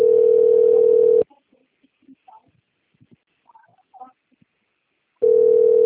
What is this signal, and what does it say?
Ringback tone (North America), 440 + 480 Hz.